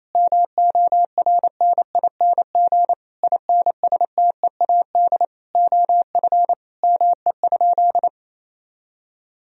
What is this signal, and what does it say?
Morse code, heard as a keyed tone, about 28 wpm, 700 Hz. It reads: MORNING INSTEAD OF ME?
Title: Sentences from Top 1000 Words - Review Part 4 - 28wpm